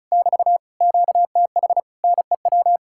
Morse code, heard as a keyed tone, about 40 wpm, 700 Hz. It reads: <BT> QTH NEW